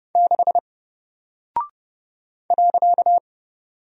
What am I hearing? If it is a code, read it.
Morse code, heard as a keyed tone, about 30 wpm, 700 Hz. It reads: 6 .
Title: Single Character (Letter:Number:Punctuation:Prosign) Alt 30wpm